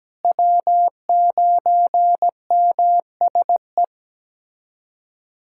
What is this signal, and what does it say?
Morse code, heard as a keyed tone, about 17 wpm, 700 Hz. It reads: W9MSE
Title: USA Callsign DE Callsign - 5 Characters - 17wpm